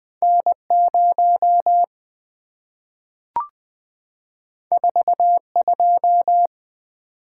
Morse code, heard as a keyed tone, about 20 wpm, 700 Hz. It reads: N0 42
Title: Two Letter-Number Combinations - 20wpm